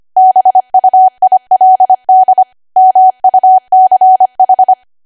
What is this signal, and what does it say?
Morse code, heard as a keyed tone, about 25 wpm, 750 Hz. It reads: BUILD MUCH
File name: Archive / 25 WPM / 25WPM